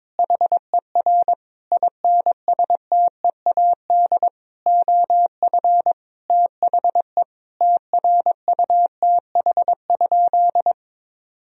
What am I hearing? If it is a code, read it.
Morse code, heard as a keyed tone, about 22 wpm, 700 Hz. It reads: HER INSTEAD OF THE TRUTH?
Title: Sentences from Top 1000 Words - Review Part 4 - 22wpm